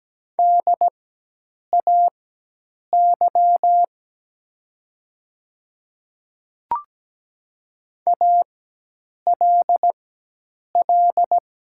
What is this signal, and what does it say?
Morse code, heard as a keyed tone, about 17 wpm, spaced as for 8 wpm, 700 Hz. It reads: DAY ALL
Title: 4x Character Spacing - Top 200 Words - 17wpm